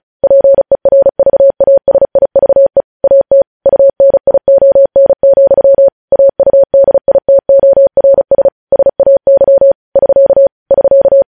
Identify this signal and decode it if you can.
Morse code, heard as a keyed tone, about 36 wpm, 550 Hz. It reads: PERVASIVE AT UNION, AUDITORS SAY <SK> <SK>
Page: News Headlines in Morse Code at 35 WPM